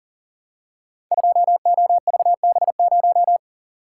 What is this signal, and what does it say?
Morse code, heard as a keyed tone, about 39 wpm, 700 Hz. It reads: 2OVB0
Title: Code Groups - 5 Letter:Numbers - 40wpm